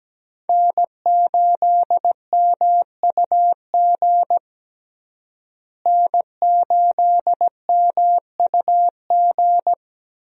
Morse code, heard as a keyed tone, about 17 wpm, 700 Hz. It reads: N8MUG N8MUG
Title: USA Call Signs - 5 Characters - Repeated 3x - 17wpm